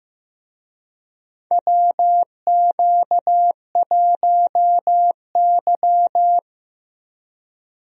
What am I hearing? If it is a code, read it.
Morse code, heard as a keyed tone, about 15 wpm, 700 Hz. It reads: WQ1Y
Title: USA Call Signs - 4 Characters - Repeated 2x - 15wpm